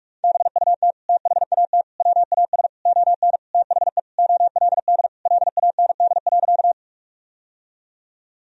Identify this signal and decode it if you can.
Morse code, heard as a keyed tone, about 46 wpm, 700 Hz. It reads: BUT THAT WAS ON THE OLD LAND.
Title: Sentences from Top 1000 Words - Review Part 1 - 45wpm